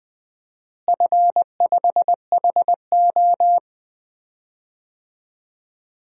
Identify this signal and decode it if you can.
Morse code, heard as a keyed tone, about 20 wpm, 700 Hz. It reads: F5HO
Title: DXCC-100 Call Signs - 4 Characters - Repeated 3x - 20wpm